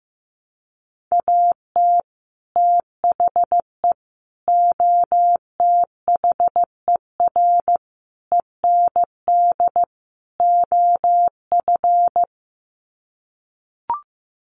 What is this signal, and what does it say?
Morse code, heard as a keyed tone, about 15 wpm, 700 Hz. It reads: AT THE OTHER END OF
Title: Sets of 5 Words 15wpm